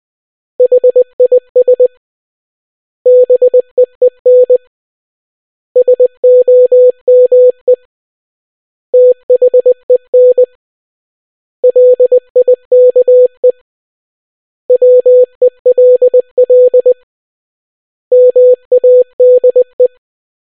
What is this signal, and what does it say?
Morse code, heard as a keyed tone, about 20 wpm, 500 Hz. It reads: HIS BEEN SOME THEN LIKE WELL MADE